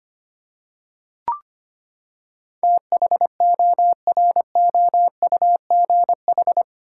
Morse code, heard as a keyed tone, about 25 wpm, 700 Hz. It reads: THOROUGH
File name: Eight Letter Words - 25wpm